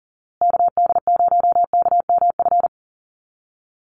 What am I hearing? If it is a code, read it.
Morse code, heard as a keyed tone, about 40 wpm, 700 Hz. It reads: KD0KMF
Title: USA Call Signs Repeated 2x 40wpm